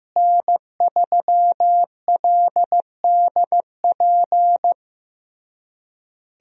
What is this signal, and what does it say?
Morse code, heard as a keyed tone, about 15 wpm, 700 Hz. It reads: N3LDP
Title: USA Call Signs - 5 Characters - Repeated 3x - 15wpm